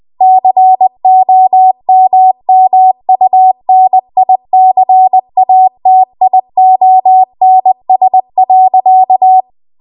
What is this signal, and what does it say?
Morse code, heard as a keyed tone, about 20 wpm, 750 Hz. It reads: COMMUNICATIONS.